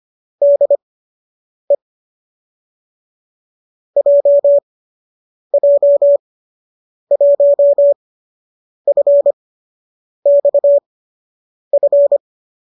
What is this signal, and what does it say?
Morse code, heard as a keyed tone, about 25 wpm, spaced as for 8 wpm, 570 Hz. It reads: DE JJ1FXF